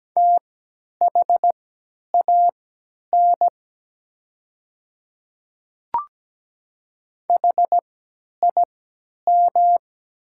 Morse code, heard as a keyed tone, about 17 wpm, spaced as for 10 wpm, 700 Hz. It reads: THAN HIM
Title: 3x Character Spacing - Top 100 Words - 17wpm